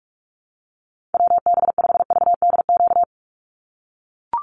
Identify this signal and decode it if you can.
Morse code, heard as a keyed tone, about 44 wpm, 700 Hz. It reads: WB5VDQ
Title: USA Call Signs 45wpm